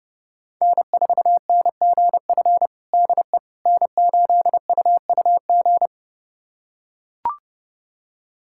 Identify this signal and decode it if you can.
Morse code, heard as a keyed tone, about 31 wpm, 700 Hz. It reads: N4NGF DE N8UUG
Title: USA Callsign DE Callsign - 5 Characters - 30wpm